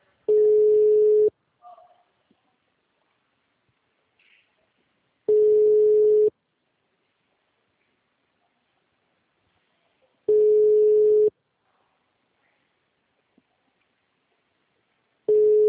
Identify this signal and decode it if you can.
Ringback tone (Europe, 425 Hz).
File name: out- -2019- - - .17